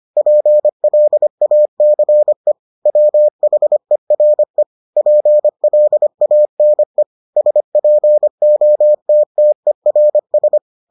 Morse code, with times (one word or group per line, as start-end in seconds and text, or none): 0.2-2.5 PLACE
2.9-4.6 WHERE
5.0-7.0 PLANE
7.4-10.6 SPOTTERS